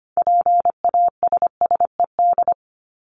Morse code, heard as a keyed tone, about 25 wpm, 700 Hz. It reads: PASSED
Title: ED Suffix - Top 100 Words with Base Words 25wpm